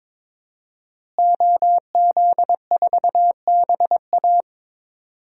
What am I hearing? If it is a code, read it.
Morse code, heard as a keyed tone, about 22 wpm, 700 Hz. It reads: OZ4BA
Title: DXCC-100 Call Signs - 5 Characters - Repeated 3x - 22wpm